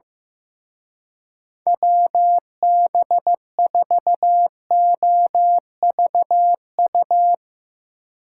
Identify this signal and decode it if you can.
Morse code, heard as a keyed tone, about 16 wpm, 700 Hz. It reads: WB4OVU
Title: Speed Racing - USA 2x3 Call Signs 15wpm